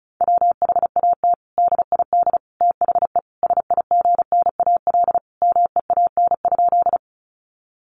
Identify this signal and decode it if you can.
Morse code, heard as a keyed tone, about 34 wpm, 700 Hz. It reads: WHAT DID THE SIGNAL MEAN?
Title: Sentences from Top 1400 Words 35wpm